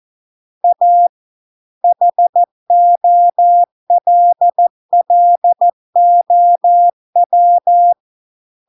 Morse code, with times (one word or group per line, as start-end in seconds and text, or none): 0.6-1.1 A
1.8-7.9 HOLLOW